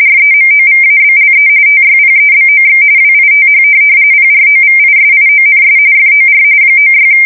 RTTY, about 45.45 baud. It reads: REMOVABLE TYPEBOX WITH PROVISIONS FOR 96
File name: fsk170 / teletype-corp